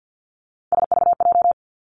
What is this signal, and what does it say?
Morse code, heard as a keyed tone, about 50 wpm, 700 Hz. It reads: SVJ
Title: Three Challenging Letter-Number Combinations 50wpm